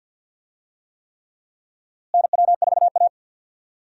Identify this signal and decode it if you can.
Morse code, heard as a keyed tone, about 52 wpm, 700 Hz. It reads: NW4A
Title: USA Call Signs - 4 Characters - Repeated 3x - 50wpm